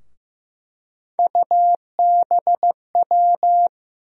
Morse code, heard as a keyed tone, about 16 wpm, 700 Hz. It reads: UBW
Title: USA Call Sign Postfixes - 3 Letters 15wpm